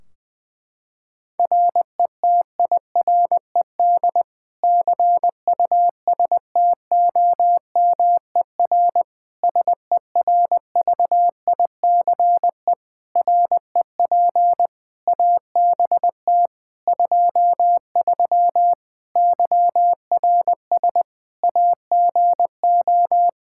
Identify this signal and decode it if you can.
Morse code, heard as a keyed tone, about 21 wpm, 700 Hz. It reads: RETIRED CUSTOMER SERVICE REP ABT 23 YRS AGO
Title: K1USN SST QSO Practice 20wpm